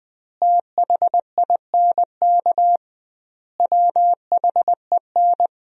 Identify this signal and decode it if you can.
Morse code, heard as a keyed tone, about 21 wpm, 700 Hz. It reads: THINK WHEN
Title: 2x Word-Spacing - Sentences from Top 300 Words - 20wpm